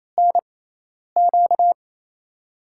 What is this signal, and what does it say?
Morse code, heard as a keyed tone, about 28 wpm, spaced as for 10 wpm, 700 Hz. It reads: NQ
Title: USA Call Signs - 5 Characters - 6x Character Spacing - 28wpm